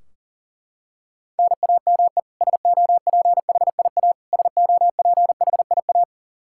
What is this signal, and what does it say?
Morse code, heard as a keyed tone, about 40 wpm, 700 Hz. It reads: NAME SOPHIA SOPHIA
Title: Practice QSO - Exchange 1 - Beginning - 40wpm